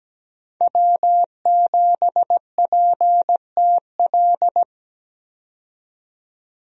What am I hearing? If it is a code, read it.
Morse code, heard as a keyed tone, about 17 wpm, 700 Hz. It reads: W7PTL
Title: USA Call Signs - 5 Characters - 17wpm